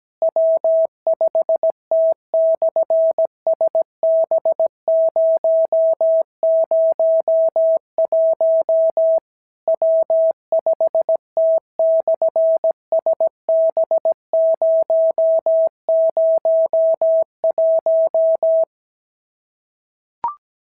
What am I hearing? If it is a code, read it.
Morse code, heard as a keyed tone, about 17 wpm, 650 Hz. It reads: W5T/SB001 W5T/SB001
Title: SOTA QSO Practice 17wpm